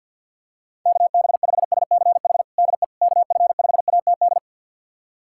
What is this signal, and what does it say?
Morse code, heard as a keyed tone, about 50 wpm, 700 Hz. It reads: KB5SCH DE KF5ATD
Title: USA Callsign DE Callsign - 6 Characters - 50wpm